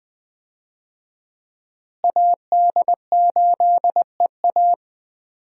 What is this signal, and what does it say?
Morse code, heard as a keyed tone, about 20 wpm, 700 Hz. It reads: AD8EA
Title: USA Call Signs - 5 Characters - Repeated 3x - 20wpm